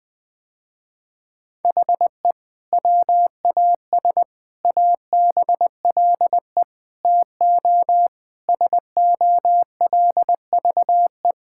Morse code, heard as a keyed tone, about 21 wpm, 700 Hz. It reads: HE WAS ABLE TO SOLVE
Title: Sentences with 3-Word Phrases - Spoken Beginning - 20wpm